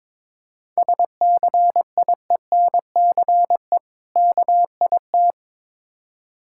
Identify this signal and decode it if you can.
Morse code, heard as a keyed tone, about 22 wpm, 690 Hz. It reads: SCIENCE KIT
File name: Difficult Words with Hints - 22wpm